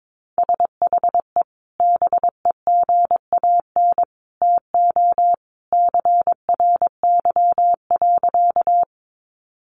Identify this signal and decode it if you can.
Morse code, heard as a keyed tone, about 22 wpm, 700 Hz. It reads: SHE BEGAN TO CRY.
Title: Sentences from Top 500 Words 22wpm